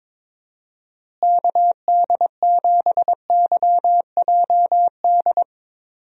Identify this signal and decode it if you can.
Morse code, heard as a keyed tone, about 22 wpm, 700 Hz. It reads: KD7YJD
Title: USA Call Signs - 6 Characters - Repeated 2x - 22wpm